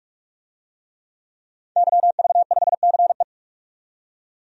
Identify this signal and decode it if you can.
Morse code, heard as a keyed tone, about 46 wpm, 700 Hz. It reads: YV5CE
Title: DXCC-100 Call Signs - 5 Characters - 45wpm